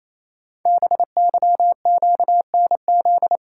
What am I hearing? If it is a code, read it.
Morse code, heard as a keyed tone, about 29 wpm, 700 Hz. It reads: BYQNZ
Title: Code Groups - 5 Letters - 28wpm